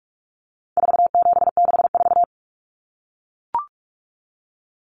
Morse code, heard as a keyed tone, about 44 wpm, 700 Hz. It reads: 4764
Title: Numbers - 4-Digit 45wpm